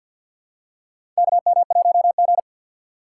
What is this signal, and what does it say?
Morse code, heard as a keyed tone, about 52 wpm, 700 Hz. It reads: KM1G
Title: USA Call Signs - 4 Characters - 2x Character Spacing - 50wpm